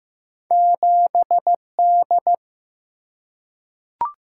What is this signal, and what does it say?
Morse code, heard as a keyed tone, about 15 wpm, 700 Hz. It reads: 7D
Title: Single-Letter Single-Number 15wpm